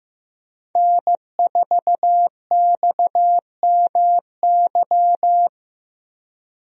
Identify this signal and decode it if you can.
Morse code, heard as a keyed tone, about 15 wpm, 700 Hz. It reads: N4XMY
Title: USA Call Signs - 5 Characters - Repeated 3x - 15wpm